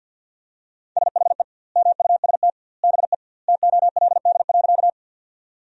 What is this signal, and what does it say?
Morse code, heard as a keyed tone, about 50 wpm, 700 Hz. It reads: SHE MUST BE TOLD.
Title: 2x Word-Spacing - Sentences from Top 400 Words - 50wpm